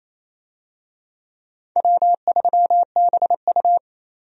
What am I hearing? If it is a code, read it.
Morse code, heard as a keyed tone, about 28 wpm, 700 Hz. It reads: W3BU
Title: USA Call Signs - 4 Characters - 28wpm